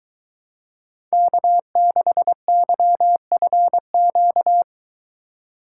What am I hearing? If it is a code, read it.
Morse code, heard as a keyed tone, about 23 wpm, 700 Hz. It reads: K6YFQ